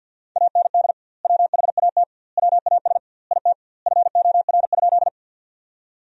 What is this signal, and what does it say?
Morse code, heard as a keyed tone, about 50 wpm, 700 Hz. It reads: AND WHAT WAS IT FOR?
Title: 2x Word-Spacing - Sentences from Top 100 Words - 50wpm